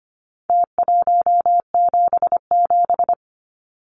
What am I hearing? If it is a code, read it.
Morse code, heard as a keyed tone, about 25 wpm, 700 Hz. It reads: T177
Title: Numbers - Contest Style - 4-Digit 25wpm